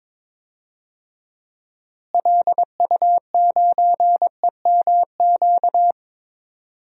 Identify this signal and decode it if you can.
Morse code, heard as a keyed tone, about 22 wpm, 700 Hz. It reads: LU9EMQ
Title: DXCC-100 Call Signs - 6 Characters - Repeated 2x - 22wpm